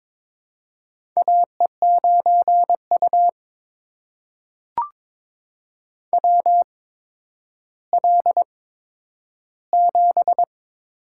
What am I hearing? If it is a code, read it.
Morse code, heard as a keyed tone, about 22 wpm, spaced as for 11 wpm, 700 Hz. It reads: AE9U WL7
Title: USA Call Signs - 8x Character Spacing - 22wpm